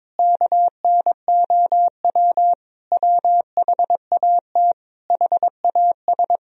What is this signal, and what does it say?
Morse code, heard as a keyed tone, about 22 wpm, 700 Hz. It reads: KNOW WHAT HAS
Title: Sentences from Top 1000 Words - Review Part 1 - 22wpm